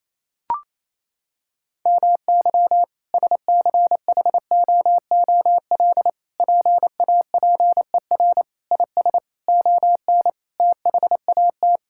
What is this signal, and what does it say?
Morse code, heard as a keyed tone, about 28 wpm, 700 Hz. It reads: MY SCHOOL PAPER IS ON THAT
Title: Sentences from Top 1000 Words - Review Part 6 - 28wpm